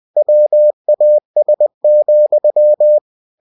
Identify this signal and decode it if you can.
Morse code, heard as a keyed tone, about 20 wpm, 600 Hz. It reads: WAS,
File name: cw-books / winnie-the-pooh / winnie-20wpm-page-0020